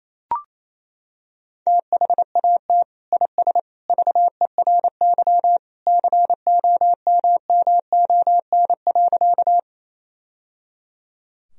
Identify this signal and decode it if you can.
Morse code, heard as a keyed tone, about 28 wpm, 700 Hz. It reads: THAT IS VERY COMMON.
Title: Sentences from Top 1000 Words - Review Part 3 - 28wpm